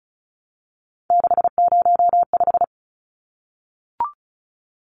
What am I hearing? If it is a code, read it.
Morse code, heard as a keyed tone, about 34 wpm, 700 Hz. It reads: 605
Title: Numbers - 3-Digit 35wpm